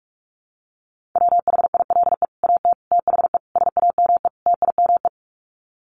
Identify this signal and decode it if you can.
Morse code, heard as a keyed tone, about 44 wpm, 700 Hz. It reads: WHILE AT THE SAME TIME
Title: Sets of 5 Words 45wpm